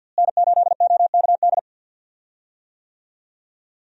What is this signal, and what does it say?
Morse code, heard as a keyed tone, about 51 wpm, 700 Hz. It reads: N8OKD
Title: USA Call Signs - 5 Characters - Repeated 2x - 50wpm